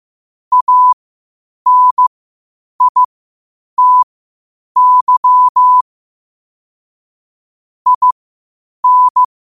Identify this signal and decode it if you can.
Morse code, heard as a keyed tone, about 15 wpm, spaced as for 8 wpm, 1000 Hz. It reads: ANITY IN
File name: qotdfiles / 8wpm